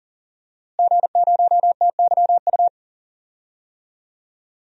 Morse code, heard as a keyed tone, about 40 wpm, 700 Hz. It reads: G0TYU
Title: DXCC-100 Call Signs - 5 Characters - Repeated 2x - 40wpm